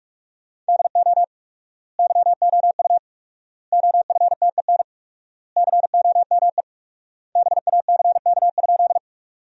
Morse code, heard as a keyed tone, about 44 wpm, 700 Hz. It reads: DO YOU OFTEN COME BACK?
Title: 4x Word-Spacing - Sentences from Top 300 Words - 45wpm